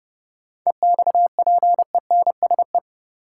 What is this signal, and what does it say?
Morse code, heard as a keyed tone, about 30 wpm, 700 Hz. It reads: EXPENSE
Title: Sets of 4 Words - New Word Drill - 30wpm